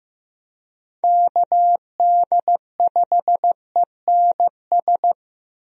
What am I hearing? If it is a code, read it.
Morse code, heard as a keyed tone, about 16 wpm, 700 Hz. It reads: KD5ENS
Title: USA Call Signs - 6 Characters - Repeated 2x - 15wpm